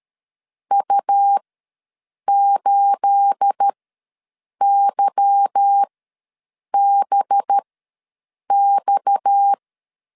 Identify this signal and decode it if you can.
Morse code, heard as a keyed tone, about 13 wpm, spaced as for 7 wpm, 800 Hz. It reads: U8YBX
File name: LEZIONI / 10 SIGNS / 10A SLASH ws 12wpm 3443